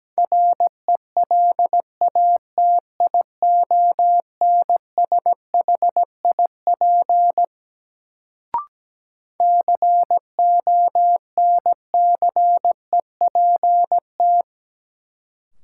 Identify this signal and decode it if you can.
Morse code, heard as a keyed tone, about 17 wpm, 700 Hz. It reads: RELATIONSHIP CONCEPT
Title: Sentences from Top 1100 Words - New Word Drill Alt - 17wpm